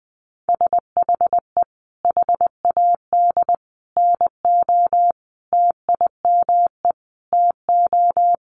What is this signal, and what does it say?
Morse code, heard as a keyed tone, about 20 wpm, 700 Hz. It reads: SHE HAD NO TIME TO
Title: Sentences from Top 1100 Words 20wpm